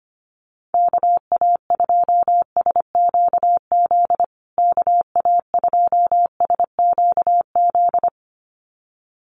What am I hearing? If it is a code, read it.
Morse code, heard as a keyed tone, about 25 wpm, 700 Hz. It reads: KA2SQZ KA2SQZ
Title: USA Call Signs Repeated 2x 25wpm